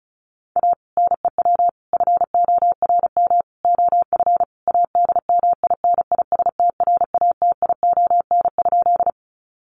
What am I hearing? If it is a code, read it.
Morse code, heard as a keyed tone, about 34 wpm, 700 Hz. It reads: A NEW FORM OF ADMINISTRATION?
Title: Sentences from Top 1300 Words 35wpm